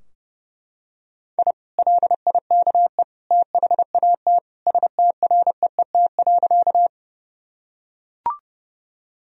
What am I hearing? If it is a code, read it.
Morse code, heard as a keyed tone, about 30 wpm, 700 Hz. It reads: I LIKE THAT STREET.
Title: Sentences from Top 1000 Words - Review Part 3 - 30wpm